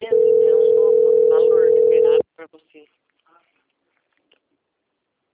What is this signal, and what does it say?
Ringback tone (North America), 440 + 480 Hz, with speech.
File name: out- - - - - .23944